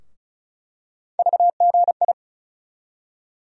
Morse code, heard as a keyed tone, about 36 wpm, 700 Hz. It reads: VGI
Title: USA Call Sign Postfixes - 3 Letters 35wpm